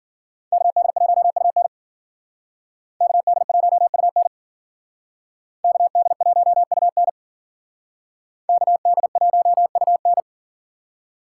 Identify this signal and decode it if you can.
Morse code, heard as a keyed tone, about 51 wpm, 700 Hz. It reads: KD1UN KD1UN KD1UN KD1UN